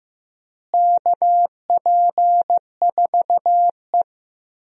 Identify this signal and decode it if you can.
Morse code, heard as a keyed tone, about 15 wpm, 700 Hz. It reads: KP4E
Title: DXCC-100 Call Signs - 4 Characters - Repeated 3x - 15wpm